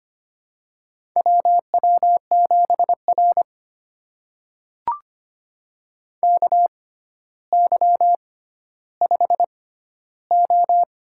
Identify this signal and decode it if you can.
Morse code, heard as a keyed tone, about 25 wpm, spaced as for 13 wpm, 700 Hz. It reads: WW7R KY5O
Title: USA Call Signs - 4 Characters - 6x Character Spacing - 25wpm